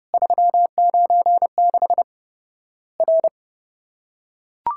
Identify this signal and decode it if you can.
Morse code, heard as a keyed tone, about 30 wpm, 700 Hz. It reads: 396 R
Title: Worked All Europe DX Contest - QTC Exchange - 30wpm